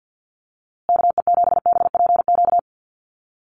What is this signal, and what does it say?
Morse code, heard as a keyed tone, about 50 wpm, 700 Hz. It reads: KE7BPQ
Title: USA Call Signs Repeated 2x 50wpm